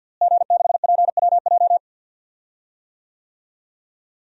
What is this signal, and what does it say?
Morse code, heard as a keyed tone, about 47 wpm, 700 Hz. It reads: G6PWJ
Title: DXCC-100 Call Signs - 5 Characters - Repeated 3x - 50wpm